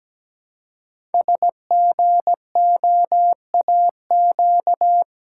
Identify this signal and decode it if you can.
Morse code, heard as a keyed tone, about 17 wpm, 700 Hz. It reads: SGOAQ
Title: Code Groups - 5 Letters - 17wpm